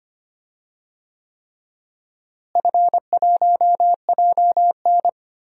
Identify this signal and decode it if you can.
Morse code, heard as a keyed tone, about 25 wpm, 700 Hz. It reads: F1JN
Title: DXCC-100 Call Signs - 4 Characters - Repeated 3x - 25wpm